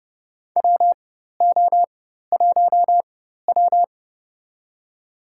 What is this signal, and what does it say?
Morse code, heard as a keyed tone, about 30 wpm, spaced as for 14 wpm, 700 Hz. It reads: WO1W
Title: USA Call Signs - 4 Characters - 4x Character Spacing - 30wpm